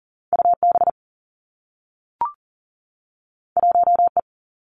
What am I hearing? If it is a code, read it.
Morse code, heard as a keyed tone, about 39 wpm, 700 Hz. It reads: UB 1E
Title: Two Challenging Letter-Number Combinations 40wpm